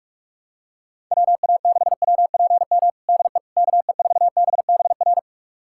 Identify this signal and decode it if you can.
Morse code, heard as a keyed tone, about 43 wpm, 700 Hz. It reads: WA6WPM DE KE4BBR
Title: USA Callsign DE Callsign - 6 Characters - 45wpm